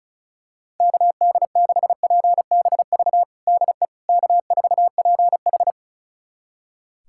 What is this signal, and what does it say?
Morse code, heard as a keyed tone, about 36 wpm, 700 Hz. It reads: KD6PBV DE K4PH
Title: USA Callsign DE Callsign 35wpm